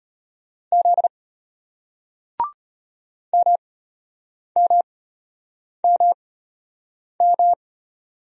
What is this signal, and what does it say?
Morse code, heard as a keyed tone, about 35 wpm, spaced as for 8 wpm, 700 Hz. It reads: Z MMMM